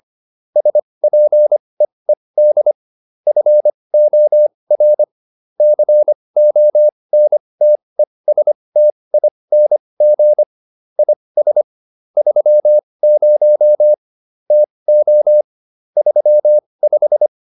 Morse code, heard as a keyed tone, about 25 wpm, spaced as for 20 wpm, 600 Hz. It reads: SPEED FOR CONTESTING IS 30 TO 35